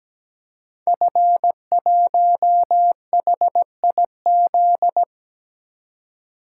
Morse code, heard as a keyed tone, about 17 wpm, 700 Hz. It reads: F1HIZ
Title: Code Groups - 5 Letter:Numbers - 17wpm